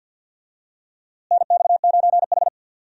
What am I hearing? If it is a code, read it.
Morse code, heard as a keyed tone, about 51 wpm, 700 Hz. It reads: NX9H
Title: USA Call Signs - 4 Characters - Repeated 3x - 50wpm